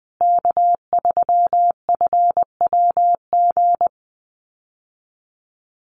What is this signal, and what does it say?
Morse code, heard as a keyed tone, about 20 wpm, 700 Hz. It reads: K3FWG
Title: USA Call Signs Repeated 3x 20wpm